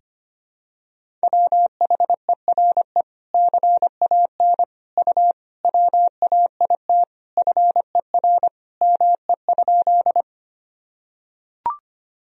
Morse code, heard as a keyed tone, about 25 wpm, 700 Hz. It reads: WHERE CAN U WAIT FER ME?
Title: Sentences from Top 500 Words and Abbreviations 25wpm